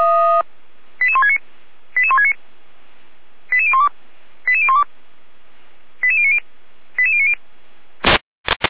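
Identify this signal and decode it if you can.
Five-tone selective call (ZVEI), heard as quick runs of five tones, tones of about 70 ms. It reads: 88169 88169 88012 88012 88090 88090
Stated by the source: Five-tone ZVEI selective call received on an SDR.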